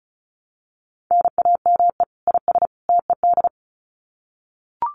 Morse code, heard as a keyed tone, about 34 wpm, 700 Hz. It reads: NAME IS TED
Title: Common QSO Phrases 35wpm